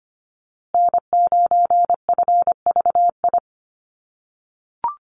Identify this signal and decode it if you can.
Morse code, heard as a keyed tone, about 25 wpm, 700 Hz. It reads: N9FVI
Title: USA Call Signs Repeated 3x 25wpm